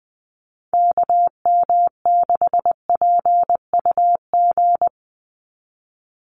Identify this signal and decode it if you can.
Morse code, heard as a keyed tone, about 20 wpm, 700 Hz. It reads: KM6PUG
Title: USA Call Signs 20wpm